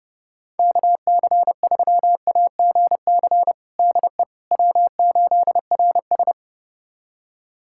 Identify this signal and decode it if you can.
Morse code, heard as a keyed tone, about 32 wpm, 700 Hz. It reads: KC3AGC DE W8RS
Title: USA Callsign DE Callsign 30wpm